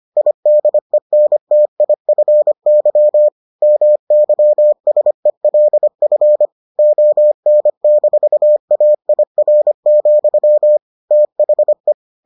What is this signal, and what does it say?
Morse code, heard as a keyed tone, about 25 wpm, 600 Hz. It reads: IDENTIFY MYSELF ON-AIR, THE